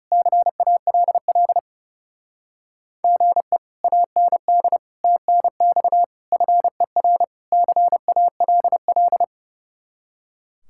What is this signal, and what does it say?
Morse code, heard as a keyed tone, about 37 wpm, 700 Hz. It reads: CALL GE AND TNX FER CALL
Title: Speed-Racing Common QSO Phrases Encore - 30wpm